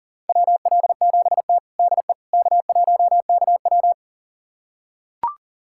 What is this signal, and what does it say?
Morse code, heard as a keyed tone, about 40 wpm, 700 Hz. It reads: WL7T DE K1KW
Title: USA Callsign DE Callsign - 4 Characters - 40wpm